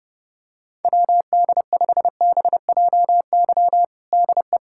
Morse code, heard as a keyed tone, about 32 wpm, 700 Hz. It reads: WD5BJY DE
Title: USA Callsign DE Callsign - 6 Characters - 30wpm